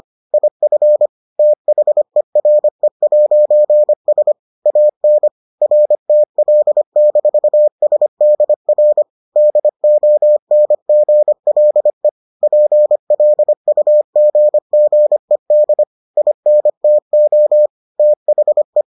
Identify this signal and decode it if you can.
Morse code, heard as a keyed tone, about 25 wpm, 600 Hz. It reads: IF THERE'S AN RTL-SDR DONGLE PLUGGED INTO THE